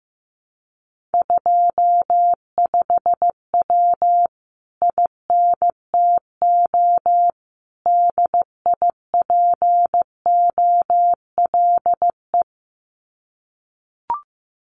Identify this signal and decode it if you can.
Morse code, heard as a keyed tone, about 15 wpm, 700 Hz. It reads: 25W INTO DIPOLE
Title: Common QSO Phrases Encore 15wpm